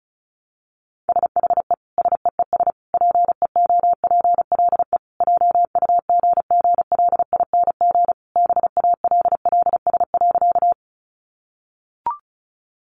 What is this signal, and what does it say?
Morse code, heard as a keyed tone, about 34 wpm, 700 Hz. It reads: SHE SEES PEOPLE JUGGLING BALLS.
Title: Easy Phrases 35wpm